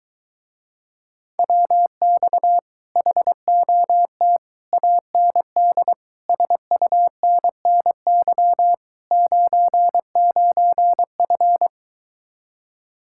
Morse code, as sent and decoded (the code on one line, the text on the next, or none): .-- -..- / .... --- - / .- -. -.. / ... ..- -. -. -.-- / ----. ----. ..-.
WX HOT AND SUNNY 99F